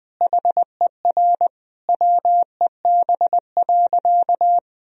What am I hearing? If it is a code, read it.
Morse code, heard as a keyed tone, about 20 wpm, 700 Hz. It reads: HER WEB.
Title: Sentences from Top 1000 Words - Review Part 4 - 20wpm